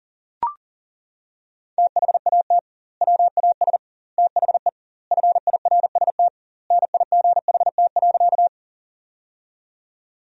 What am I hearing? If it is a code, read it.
Morse code, heard as a keyed tone, about 40 wpm, 700 Hz. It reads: THAT WAS THE FIRST NIGHT.
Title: 2x Word-Spacing - Sentences from Top 300 Words - 40wpm